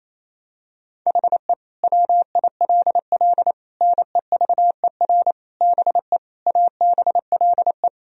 Morse code, heard as a keyed tone, about 28 wpm, 700 Hz. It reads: HE WILL NEVER BE ABLE
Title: Sentences from Top 1000 Words - Review Part 2 - 28wpm